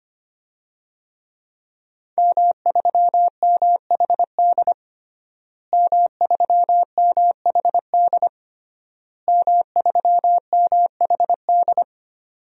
Morse code, heard as a keyed tone, about 25 wpm, 700 Hz. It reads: M3MHD M3MHD M3MHD
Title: DXCC-100 Call Signs - 5 Characters - Repeated 3x - 25wpm